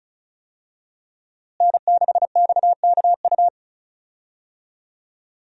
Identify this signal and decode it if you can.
Morse code, heard as a keyed tone, about 36 wpm, 700 Hz. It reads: N6XKU
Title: USA Call Signs - 5 Characters - Repeated 3x - 35wpm